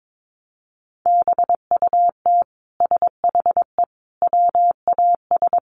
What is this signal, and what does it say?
Morse code, heard as a keyed tone, about 22 wpm, 700 Hz. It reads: BUT SHE WAS
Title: Sentences from Top 1500 Words 22wpm